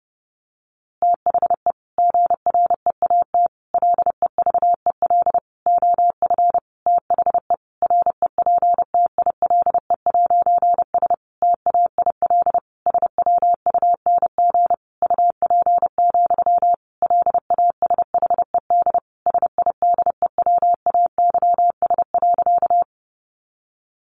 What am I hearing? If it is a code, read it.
Morse code, heard as a keyed tone, about 30 wpm, 700 Hz. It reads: THE GREAT LEVEL OF THE REPTILE'S TAIL SWUNG UP, LASHED SIDEWAYS.
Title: A Sound Of Thunder 30wpm